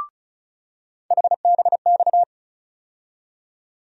Morse code, heard as a keyed tone, about 36 wpm, 700 Hz. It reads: HBX
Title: USA Call Sign Postfixes - 3 Letters 35wpm